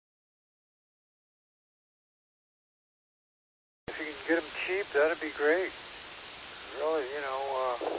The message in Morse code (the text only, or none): none